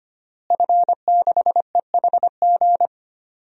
Morse code, heard as a keyed tone, about 25 wpm, 700 Hz. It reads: F6EHG
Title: DXCC-100 Call Signs - 5 Characters - Repeated 3x - 25wpm